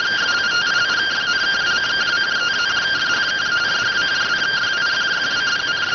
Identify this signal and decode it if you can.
Two-tone teleprinter signal, about 170 Hz shift.